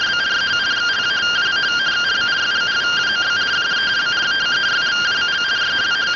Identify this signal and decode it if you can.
Two-tone teleprinter signal, about 170 Hz shift.